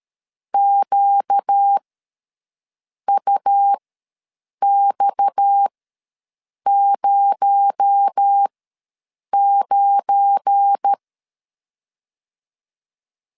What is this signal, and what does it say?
Morse code, heard as a keyed tone, about 13 wpm, spaced as for 6 wpm, 800 Hz. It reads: QUX09
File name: LEZIONI / 7 GF9Z / 7 GF9 ws 12wpm 3344